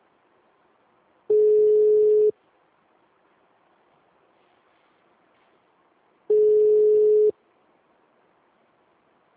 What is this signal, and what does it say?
Ringback tone (Europe, 425 Hz).
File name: out- -2015- - - .3543